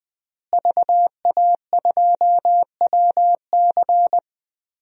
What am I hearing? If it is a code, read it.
Morse code, heard as a keyed tone, about 21 wpm, 700 Hz. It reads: VA2WC
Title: DXCC-100 Call Signs - 5 Characters - Repeated 3x - 20wpm